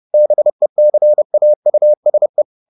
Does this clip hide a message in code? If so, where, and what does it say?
Yes: Morse code from 0.1 to 2.4 s, heard as a keyed tone, about 30 wpm, 600 Hz. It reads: BECAUSE